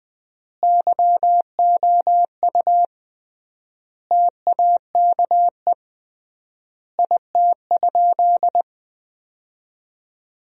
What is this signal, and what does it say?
Morse code, heard as a keyed tone, about 20 wpm, 700 Hz. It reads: YOU TAKE IT?
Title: 3x Word-Spacing - Sentences from Top 200 Words - 20wpm